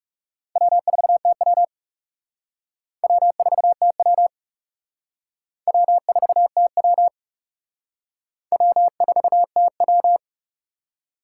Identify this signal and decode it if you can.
Morse code, heard as a keyed tone, about 37 wpm, 700 Hz. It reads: W4TW W4TW W4TW W4TW
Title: Speed Racing - USA 1x2 Call Signs 30wpm